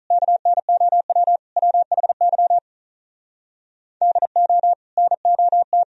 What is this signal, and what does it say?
Morse code, heard as a keyed tone, about 47 wpm, 700 Hz. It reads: KNOW WHY DO NOT